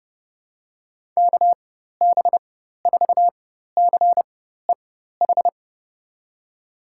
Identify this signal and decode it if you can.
Morse code, heard as a keyed tone, about 31 wpm, spaced as for 14 wpm, 700 Hz. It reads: KB4CEH
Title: USA Call Signs - 6 Characters - 4x Character Spacing - 30wpm